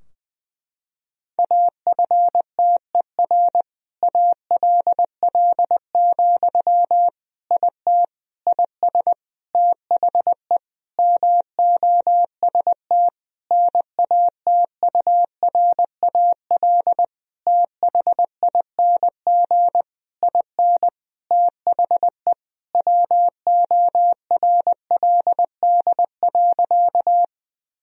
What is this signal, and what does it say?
Morse code, heard as a keyed tone, about 20 wpm, 700 Hz. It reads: AFTER ALL, IT IS THE MOST NATURAL THING IN THE WORLD.
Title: Sentences from Top 1000 Words - Review Part 4 - 20wpm